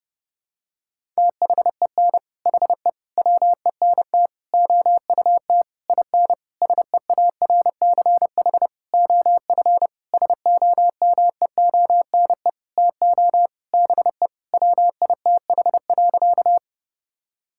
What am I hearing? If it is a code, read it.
Morse code, heard as a keyed tone, about 30 wpm, 700 Hz. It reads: THEN HE WENT OUT IN SEARCH OF SOMEONE TO BE WITH.
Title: Sentences from Top 1000 Words - Review Part 5 - 30wpm